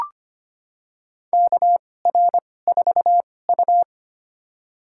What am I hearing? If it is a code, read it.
Morse code, heard as a keyed tone, about 25 wpm, spaced as for 18 wpm, 700 Hz. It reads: KR4U
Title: USA Call Signs - 4 Characters - 2x Character Spacing - 25wpm